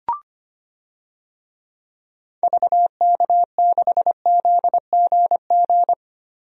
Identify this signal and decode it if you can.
Morse code, heard as a keyed tone, about 25 wpm, 700 Hz. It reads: VK6ZGG
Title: DXCC-100 Call Signs - 6 Characters - Repeated 2x - 25wpm